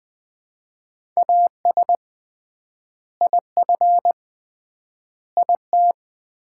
Morse code, heard as a keyed tone, about 21 wpm, 700 Hz. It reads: AS IF IT
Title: 3x Word-Spacing - Sets of 4 Words - 20wpm